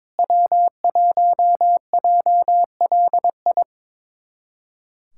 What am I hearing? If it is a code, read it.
Morse code, heard as a keyed tone, about 23 wpm, 700 Hz. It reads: W1JLI
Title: USA Callsign DE Callsign - 5 Characters - 22wpm